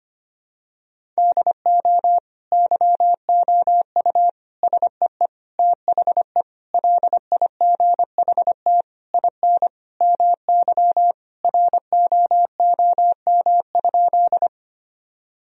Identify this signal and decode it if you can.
Morse code, heard as a keyed tone, about 25 wpm, 700 Hz. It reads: DO YOU SEE THE LIGHT IN MY ROOM?
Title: Sentences from Top 1000 Words - Review Part 1 - 25wpm